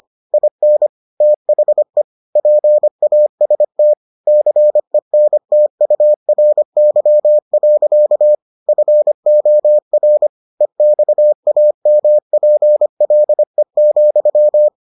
Morse code, heard as a keyed tone, about 25 wpm, 600 Hz. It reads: IN THE PAST CENTURY. FOR EXAMPLE,